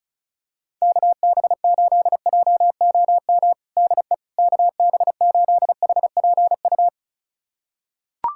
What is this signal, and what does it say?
Morse code, heard as a keyed tone, about 36 wpm, 700 Hz. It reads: KB8JOM DE KB8HPU
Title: USA Callsign DE Callsign - 6 Characters - 35wpm